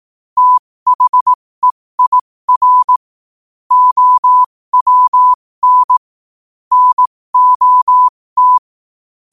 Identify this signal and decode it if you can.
Morse code, heard as a keyed tone, about 18 wpm, spaced as for 15 wpm, 1000 Hz. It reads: THEIR OWN NOT
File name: qotdfiles / 15wpm